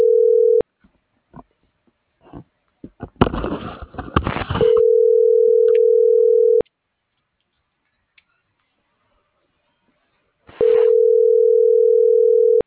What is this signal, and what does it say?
Ringback tone (North America), 440 + 480 Hz.